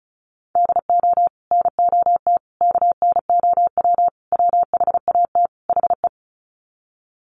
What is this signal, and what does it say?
Morse code, heard as a keyed tone, about 34 wpm, 700 Hz. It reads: DO NOT KNOW WHAT HE
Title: Sets of 5 Words Encore2 35wpm